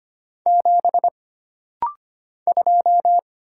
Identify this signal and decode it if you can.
Morse code, heard as a keyed tone, about 25 wpm, 700 Hz. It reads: 7 2
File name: Single Character (Letter:Number:Punctuation:Prosign) - Rapid-Fire - 25wpm